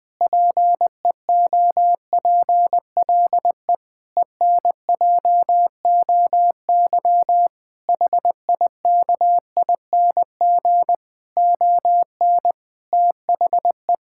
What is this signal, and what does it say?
Morse code, heard as a keyed tone, about 21 wpm, 700 Hz. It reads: PEOPLE ENJOY HIKING ON THE
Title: Difficult Sentences with Hints - 20wpm